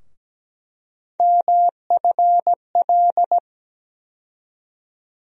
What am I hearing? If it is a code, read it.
Morse code, heard as a keyed tone, about 17 wpm, 700 Hz. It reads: MFL
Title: USA Call Sign Postfixes - 3 Letters 17wpm